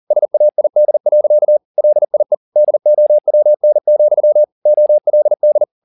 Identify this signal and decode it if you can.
Morse code, heard as a keyed tone, about 40 wpm, 600 Hz. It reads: SAID. LIE DOWN, OLD
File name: the old man and the sea 2 40wpm